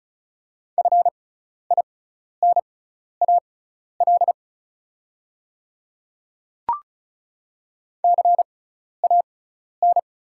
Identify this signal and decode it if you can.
Morse code, heard as a keyed tone, about 36 wpm, spaced as for 12 wpm, 700 Hz. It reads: FINAL CAN
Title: 6x Character Spacing - Top 500 Words - 35wpm